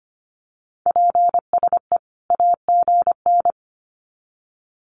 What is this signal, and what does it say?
Morse code, heard as a keyed tone, about 25 wpm, 700 Hz. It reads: PSE AGN
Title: Common QSO Phrases 25wpm